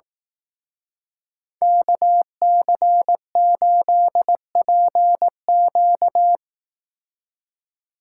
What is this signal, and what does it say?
Morse code, heard as a keyed tone, about 17 wpm, 700 Hz. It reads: KC8PQ
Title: Speed Racing - USA Call Signs 15wpm